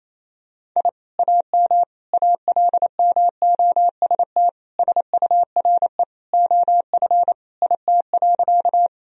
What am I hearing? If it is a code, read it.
Morse code, heard as a keyed tone, about 28 wpm, 700 Hz. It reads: I AM ALMOST SURE OF IT.
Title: Sentences from Top 1000 Words - Review Part 4 - 28wpm